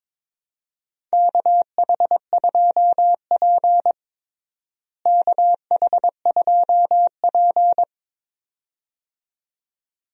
Morse code, heard as a keyed tone, about 21 wpm, 700 Hz. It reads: KH2P KH2P
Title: USA Call Signs - 4 Characters - Repeated 2x - 22wpm